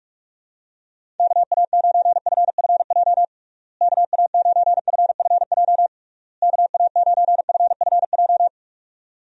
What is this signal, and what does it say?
Morse code, heard as a keyed tone, about 46 wpm, 700 Hz. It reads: KA9FFJ KA9FFJ KA9FFJ
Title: USA Call Signs - 6 Characters - Repeated 3x - 45wpm